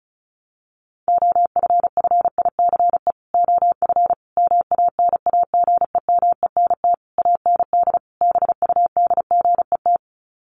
Morse code, heard as a keyed tone, about 34 wpm, 700 Hz. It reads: OFFICE OF MANAGEMENT AND BUDGET
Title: Sets of 5 Words Encore2 35wpm